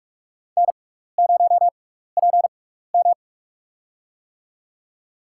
Morse code, heard as a keyed tone, about 44 wpm, spaced as for 15 wpm, 700 Hz. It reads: N0PM
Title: USA Call Signs - 4 Characters - 6x Character Spacing - 45wpm